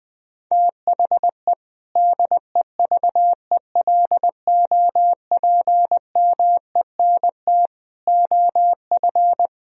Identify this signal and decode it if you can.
Morse code, heard as a keyed tone, about 21 wpm, 700 Hz. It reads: THE DEVELOPMENT OF
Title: Sentences from Top 1000 Words - Review Part 4 - 20wpm